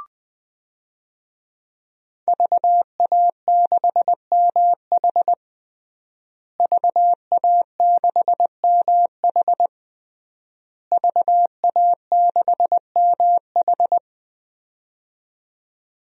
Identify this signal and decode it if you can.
Morse code, heard as a keyed tone, about 20 wpm, 700 Hz. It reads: VA6MH VA6MH VA6MH
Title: DXCC-100 Call Signs - 5 Characters - Repeated 3x - 20wpm